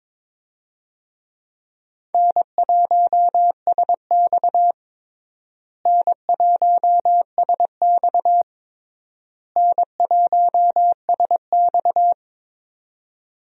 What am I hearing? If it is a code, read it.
Morse code, heard as a keyed tone, about 22 wpm, 700 Hz. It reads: N1SX N1SX N1SX
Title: USA Call Signs - 4 Characters - Repeated 3x - 22wpm